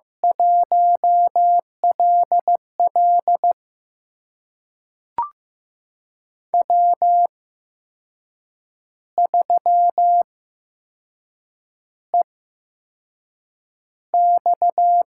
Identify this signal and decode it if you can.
Morse code, heard as a keyed tone, about 15 wpm, spaced as for 6 wpm, 700 Hz. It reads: 1LL W3EX
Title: USA Call Signs - 4 Characters - 8x Character Spacing - 15wpm